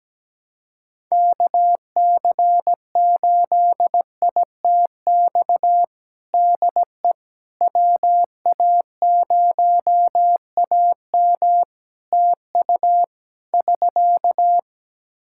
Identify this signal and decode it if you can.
Morse code, heard as a keyed tone, about 17 wpm, 700 Hz. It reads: KC8ITX DE WA0AM TU <SK>
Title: Practice QSO - Exchange 4 - Ending - 17wpm